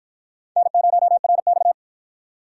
Morse code, heard as a keyed tone, about 48 wpm, 700 Hz. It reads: N0RX